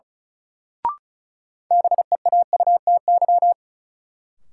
Morse code, heard as a keyed tone, about 34 wpm, 700 Hz. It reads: BEAUTY
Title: Sentences from Top 600 Words - New Word Drill - 35wpm